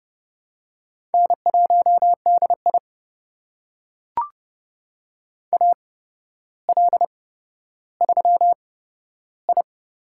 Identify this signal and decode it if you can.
Morse code, heard as a keyed tone, about 30 wpm, spaced as for 13 wpm, 700 Hz. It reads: N1DI AL3I
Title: USA Call Signs - 4 Characters - 8x Character Spacing - 30wpm